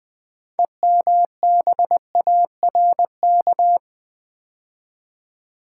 Morse code, heard as a keyed tone, about 20 wpm, 700 Hz. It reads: EMBARK
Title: 6 Letter Words - First 3 Letters Given - 20wpm